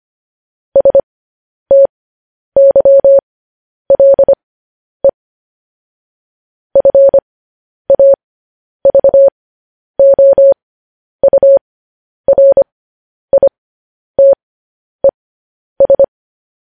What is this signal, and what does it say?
Morse code, heard as a keyed tone, about 25 wpm, spaced as for 10 wpm, 550 Hz. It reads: STYLE FAVOURITES
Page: news-headlines-in-morse-code-at-10-wpm